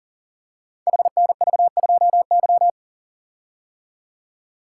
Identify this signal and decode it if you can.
Morse code, heard as a keyed tone, about 41 wpm, 700 Hz. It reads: HNV2Y
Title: Code Groups - 5 Letter:Numbers - 40wpm